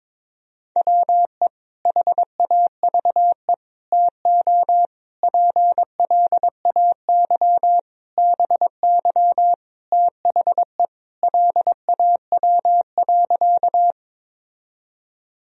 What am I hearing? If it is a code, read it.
Morse code, heard as a keyed tone, about 22 wpm, 700 Hz. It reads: WE HAVE TO PLAY BY THE LAW.
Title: Sentences from Top 1000 Words - Review Part 3 - 22wpm